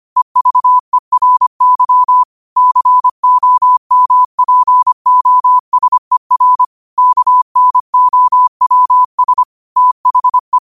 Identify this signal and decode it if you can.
Morse code, heard as a keyed tone, about 25 wpm, 1000 Hz. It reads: EVERY COMPOSER KNOWS THE
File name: qotdfiles / 25wpm